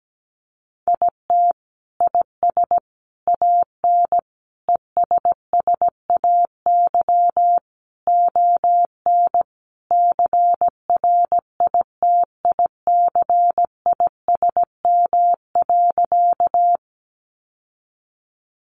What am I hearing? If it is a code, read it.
Morse code, heard as a keyed tone, about 17 wpm, 700 Hz. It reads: IT IS AN ESSAY ON CRITICISM.
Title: Sentences from Top 1500 Words 17wpm